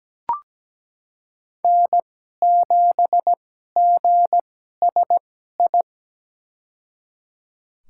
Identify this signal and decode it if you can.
Morse code, heard as a keyed tone, about 17 wpm, spaced as for 12 wpm, 700 Hz. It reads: N7GSI
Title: USA Call Signs - 5 Characters - 2x Character Spacing - 17wpm